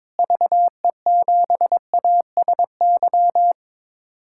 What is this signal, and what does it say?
Morse code, heard as a keyed tone, about 22 wpm, 700 Hz. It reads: VE7ASY
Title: K1USN SST QSO Practice - 22wpm